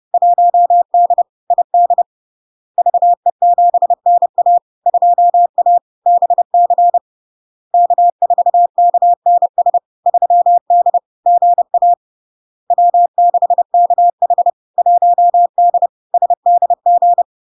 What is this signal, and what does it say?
Morse code, heard as a keyed tone, about 30 wpm, 700 Hz. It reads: 1D ID VE7NA 2A BC K4KNS 3D GA W6KH 1D SDG
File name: LARK-FD-2022-Log-in-CW-30wpm-700-Hz